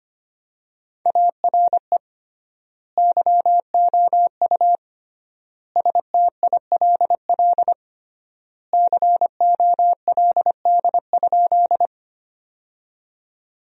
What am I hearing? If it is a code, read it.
Morse code, heard as a keyed tone, about 25 wpm, 700 Hz. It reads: ARE YOU STILL COLD?
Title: 3x Word-Spacing - Sentences from Top 400 Words - 25wpm